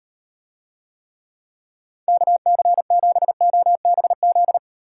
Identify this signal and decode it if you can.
Morse code, heard as a keyed tone, about 38 wpm, 700 Hz. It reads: KC7OBZ